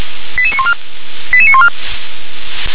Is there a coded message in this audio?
Yes: five-tone selective call (ZVEI) from 0.4 to 1.7 s, heard as quick runs of five tones, tones of about 70 ms. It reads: 88914 88914